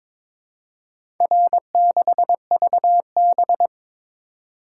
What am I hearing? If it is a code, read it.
Morse code, heard as a keyed tone, about 22 wpm, 700 Hz. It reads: R6VB
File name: practice / 22wpm / Four Letter-Number Combinations - 22wpm